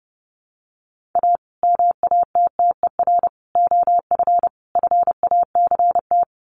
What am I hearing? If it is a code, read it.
Morse code, heard as a keyed tone, about 28 wpm, 700 Hz. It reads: A MATTER OF FACT
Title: Sets of 4 Words 30wpm